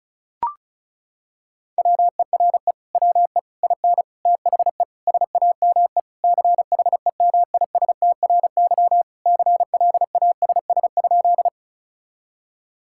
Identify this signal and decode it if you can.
Morse code, heard as a keyed tone, about 36 wpm, 700 Hz. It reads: WERE WE IN THE SAME CHEMISTRY CLASS?
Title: Sentences from Top 1000 Words - Review Part 6 - 35wpm